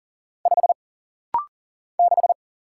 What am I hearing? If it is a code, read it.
Morse code, heard as a keyed tone, about 40 wpm, 700 Hz. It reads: 5 6
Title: Single Letter-Number - Rapid-Fire 40wpm